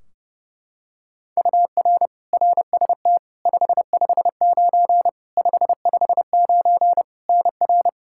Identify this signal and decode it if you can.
Morse code, heard as a keyed tone, about 32 wpm, 700 Hz. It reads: UR RST 559 559 NR
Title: Practice QSO - Exchange 1 - Beginning - 30wpm